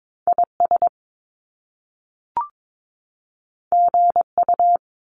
Morse code, heard as a keyed tone, about 22 wpm, 700 Hz. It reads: IS GU
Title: Top Two Letter Combinations in Words 22wpm